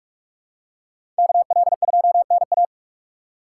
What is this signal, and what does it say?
Morse code, heard as a keyed tone, about 46 wpm, 700 Hz. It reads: KL2NA
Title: USA Call Signs - 5 Characters - 3x Character Spacing - 45wpm